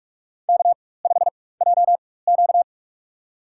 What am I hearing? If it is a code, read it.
Morse code, heard as a keyed tone, about 46 wpm, spaced as for 21 wpm, 700 Hz. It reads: K5JQ
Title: USA Call Signs - 4 Characters - 4x Character Spacing - 45wpm